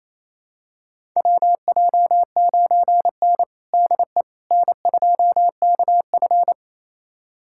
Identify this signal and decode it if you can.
Morse code, heard as a keyed tone, about 28 wpm, 700 Hz. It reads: WJ9N DE N2KF
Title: USA Callsign DE Callsign - 4 Characters - 28wpm